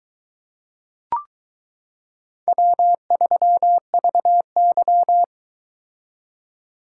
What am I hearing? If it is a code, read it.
Morse code, heard as a keyed tone, about 22 wpm, 700 Hz. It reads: W3VY